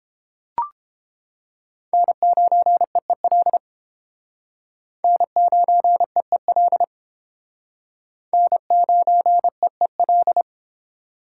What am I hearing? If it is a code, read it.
Morse code, heard as a keyed tone, about 30 wpm, 700 Hz. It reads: N9EEL N9EEL N9EEL